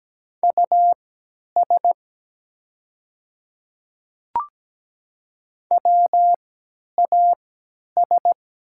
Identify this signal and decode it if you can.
Morse code, heard as a keyed tone, about 17 wpm, spaced as for 10 wpm, 690 Hz. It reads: US WAS
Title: 3x Character Spacing - Top 300 Words - 17wpm